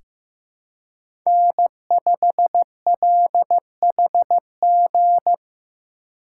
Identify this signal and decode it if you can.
Morse code, heard as a keyed tone, about 15 wpm, 700 Hz. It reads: N5LHG
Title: USA Call Signs - 3x Character Spacing - 15wpm